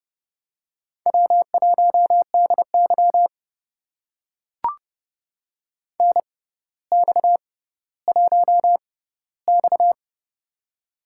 Morse code, heard as a keyed tone, about 28 wpm, spaced as for 15 wpm, 700 Hz. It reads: W1DY NX1X
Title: USA Call Signs - 4 Characters - 6x Character Spacing - 30wpm